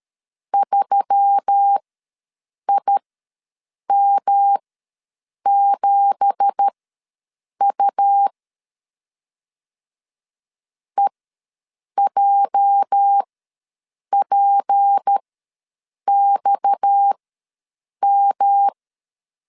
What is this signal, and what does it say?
Morse code, heard as a keyed tone, about 13 wpm, spaced as for 7 wpm, 800 Hz. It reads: 3IM7U EJPXM
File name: LEZIONI / 9 2867 ws 12wpm 3431